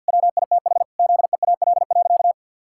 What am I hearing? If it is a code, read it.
Morse code, heard as a keyed tone, about 52 wpm, 700 Hz. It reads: WITH ZEAL.
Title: Pangram Challenge v2 - 50wpm